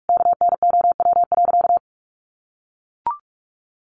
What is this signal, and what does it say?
Morse code, heard as a keyed tone, about 44 wpm, 700 Hz. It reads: KNOW.
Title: Sentences from Top 1500 Words 45wpm